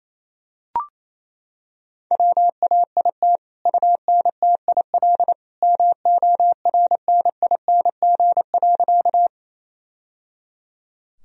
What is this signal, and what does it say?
Morse code, heard as a keyed tone, about 28 wpm, 700 Hz. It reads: WAIT UNTIL MORNING.
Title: Sentences from Top 1000 Words - Review Part 1 - 28wpm